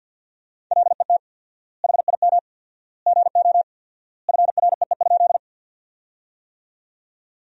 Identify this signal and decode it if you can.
Morse code, heard as a keyed tone, about 50 wpm, 700 Hz. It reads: LET HIM GO FREE?
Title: 4x Word-Spacing - Sentences from Top 500 Words - 50wpm